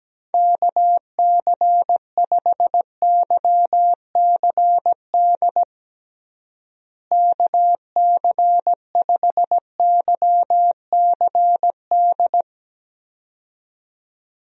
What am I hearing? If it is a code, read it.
Morse code, heard as a keyed tone, about 17 wpm, 700 Hz. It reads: KC5YCD KC5YCD
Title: USA Call Signs - 6 Characters - Repeated 2x - 17wpm